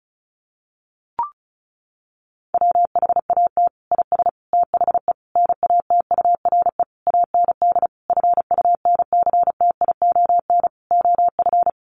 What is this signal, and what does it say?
Morse code, heard as a keyed tone, about 34 wpm, 700 Hz. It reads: WHAT IS THE NATURE AND FUNCTION OF
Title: Sentences from Top 1100 Words 35wpm